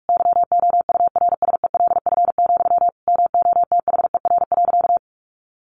Morse code, heard as a keyed tone, about 44 wpm, 700 Hz. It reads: YOURSELF, MOTHER.
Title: Sentences from Top 1300 Words 45wpm